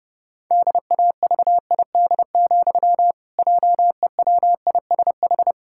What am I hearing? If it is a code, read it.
Morse code, heard as a keyed tone, about 30 wpm, 700 Hz. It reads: DAVID, JEWISH
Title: Pangram Challenge v2 - 30wpm